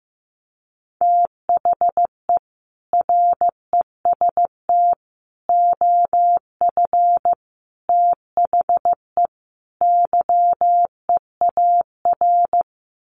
Morse code, heard as a keyed tone, about 15 wpm, 700 Hz. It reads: THE REST OF THE YEAR
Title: Sets of 5 Words Encore 15wpm